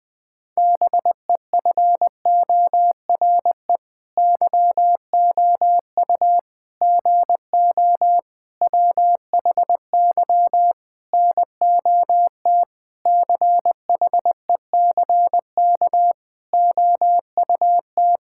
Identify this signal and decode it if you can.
Morse code, heard as a keyed tone, about 20 wpm, 700 Hz. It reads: BEFORE YOU GO WHY NOT CHECK OUT
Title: Sentences with 9-Word Phrases - 20wpm